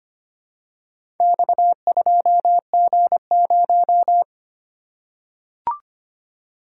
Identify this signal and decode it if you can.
Morse code, heard as a keyed tone, about 25 wpm, 700 Hz. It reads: X2G0
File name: practice / 25wpm / Four Letter-Number Combinations - 25wpm